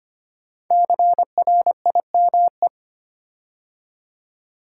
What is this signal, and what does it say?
Morse code, heard as a keyed tone, about 25 wpm, 700 Hz. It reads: CRIME
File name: 5 Letter Words - First 3 Letters Given - 25wpm